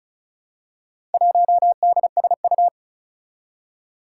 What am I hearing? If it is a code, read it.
Morse code, heard as a keyed tone, about 36 wpm, 700 Hz. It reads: 1DSU
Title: Four Challenging Letter-Number Combinations 35wpm